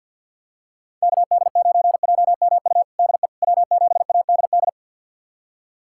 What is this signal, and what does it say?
Morse code, heard as a keyed tone, about 51 wpm, 700 Hz. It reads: KD9JMU DE W7ADD
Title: USA Callsign DE Callsign 50wpm